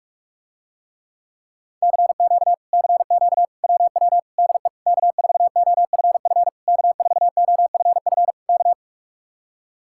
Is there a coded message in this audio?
Yes: Morse code from 1.8 to 8.7 s, heard as a keyed tone, about 50 wpm, 700 Hz. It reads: CQ CQ WW DE K4OFF K4OFF K